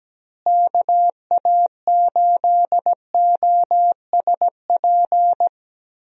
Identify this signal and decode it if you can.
Morse code, heard as a keyed tone, about 17 wpm, 700 Hz. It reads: KA8OSP
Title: USA Call Signs - 6 Characters - Repeated 2x - 17wpm